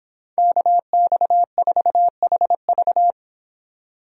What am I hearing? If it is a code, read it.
Morse code, heard as a keyed tone, about 30 wpm, 700 Hz. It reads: KX4HV